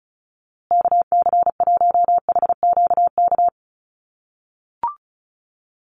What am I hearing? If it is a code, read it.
Morse code, heard as a keyed tone, about 34 wpm, 700 Hz. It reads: KC1HQK
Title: USA Call Signs Repeated 3x 35wpm